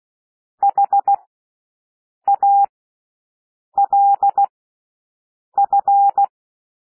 Morse code, heard as a keyed tone, about 16 wpm, spaced as for 6 wpm, 800 Hz. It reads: HALF
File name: cwmp3 / W1 SOUND WORDS 16 wpm